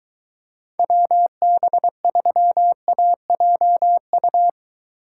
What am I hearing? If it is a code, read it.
Morse code, heard as a keyed tone, about 21 wpm, 700 Hz. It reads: WB3AJU